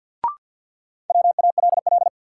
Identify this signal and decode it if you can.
Morse code, heard as a keyed tone, about 53 wpm, 700 Hz. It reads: WALL
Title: Sentences from Top 1100 Words - New Word Drill Alt - 50wpm